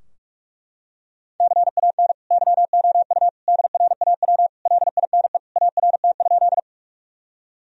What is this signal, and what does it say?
Morse code, heard as a keyed tone, about 43 wpm, 700 Hz. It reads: CAN YOU DRAW LINE ART?
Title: Sentences from Top 1000 Words - Review Part 3 - 45wpm